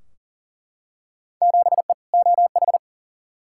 Morse code, heard as a keyed tone, about 40 wpm, 700 Hz. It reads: 7E OH
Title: Field Day Exchanges Encore - 40wpm